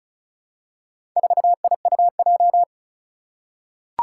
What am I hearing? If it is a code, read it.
Morse code, heard as a keyed tone, about 36 wpm, 690 Hz. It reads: 4IUJ
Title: Four Challenging Letter-Number Combinations 35wpm